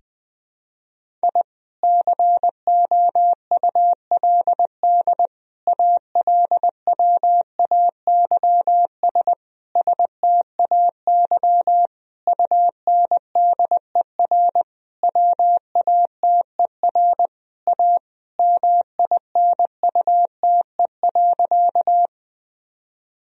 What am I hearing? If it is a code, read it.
Morse code, heard as a keyed tone, about 20 wpm, 700 Hz. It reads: I COULD ALWAYS STAY UNDER WATER A MINUTE.
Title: Sentences from Top 1000 Words - Review Part 1 - 20wpm